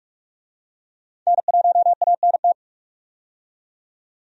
Morse code, heard as a keyed tone, about 42 wpm, 700 Hz. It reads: N1ANT
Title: USA Call Signs - 5 Characters - 45wpm